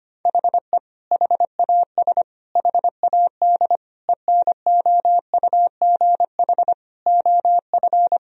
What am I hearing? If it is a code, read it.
Morse code, heard as a keyed tone, about 25 wpm, 700 Hz. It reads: HE HAS HAD ENOUGH OF
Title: Sentences from Top 1000 Words - Review Part 3 - 25wpm